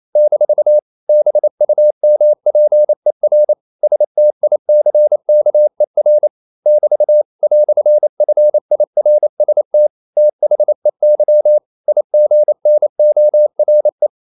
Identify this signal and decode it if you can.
Morse code, heard as a keyed tone, about 28 wpm, 600 Hz. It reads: - BUMPER STICKER <BT> "FIRST THEY IGNORE
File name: CW / Practice / EnglishQuotations0228